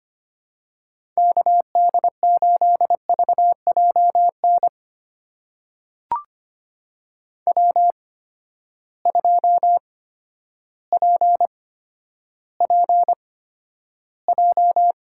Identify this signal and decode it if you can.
Morse code, heard as a keyed tone, about 25 wpm, spaced as for 11 wpm, 700 Hz. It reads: KD8VJN W2PPJ
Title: USA Call Signs - 8x Character Spacing - 25wpm